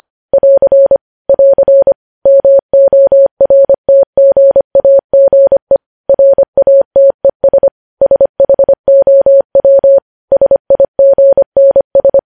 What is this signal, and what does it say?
Morse code, heard as a keyed tone, about 25 wpm, 550 Hz. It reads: <AR> <AR> MORTGAGE RATES SHOW SIGNS
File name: podcast / News headlines-1x-25x25